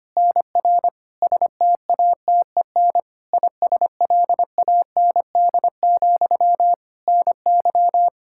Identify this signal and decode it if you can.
Morse code, heard as a keyed tone, about 25 wpm, 700 Hz. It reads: NR STATEN ISLAND, NY
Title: Practice QSO - Exchange 1 - Beginning - 25wpm